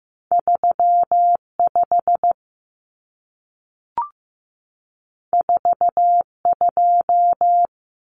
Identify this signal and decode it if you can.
Morse code, heard as a keyed tone, about 15 wpm, 700 Hz. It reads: 35 42
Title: Common Cut Numbers - 2-Digit 15wpm